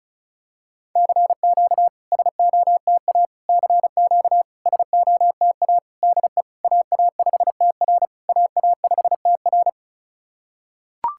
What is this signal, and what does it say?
Morse code, heard as a keyed tone, about 36 wpm, 700 Hz. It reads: CQ SOTA CQ SOTA DE AA5TR AA5TR
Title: SOTA QSO Practice 35wpm